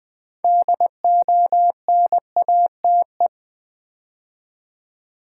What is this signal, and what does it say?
Morse code, heard as a keyed tone, about 20 wpm, 700 Hz. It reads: DONATE
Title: Build A Word - 2 Letters At A Time - 6-Letter Words - 20wpm